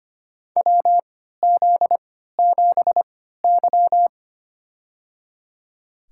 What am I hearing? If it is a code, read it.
Morse code, heard as a keyed tone, about 25 wpm, spaced as for 14 wpm, 700 Hz. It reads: WZ7Y
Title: USA Call Signs - 4 Characters - 3x Character Spacing - 25wpm